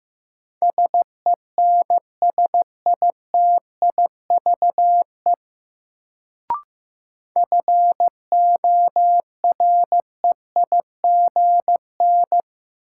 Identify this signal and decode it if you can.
Morse code, heard as a keyed tone, about 15 wpm, 700 Hz. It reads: SENSITIVE FOREIGN
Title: Sentences from Top 1400 Words - New Word Drill - 15wpm